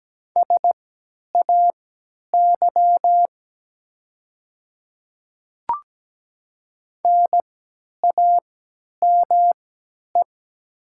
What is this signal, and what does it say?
Morse code, heard as a keyed tone, about 17 wpm, spaced as for 10 wpm, 700 Hz. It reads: SAY NAME
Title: 3x Character Spacing - Top 300 Words - 17wpm